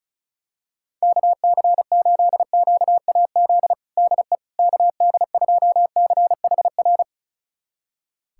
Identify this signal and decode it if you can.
Morse code, heard as a keyed tone, about 36 wpm, 700 Hz. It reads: KC8QAZ DE KD2CHR
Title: USA Callsign DE Callsign - 6 Characters - 35wpm